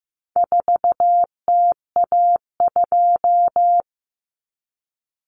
Morse code, heard as a keyed tone, about 15 wpm, 700 Hz. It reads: 4TA2
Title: Common Cut Numbers - 4-Digit 15wpm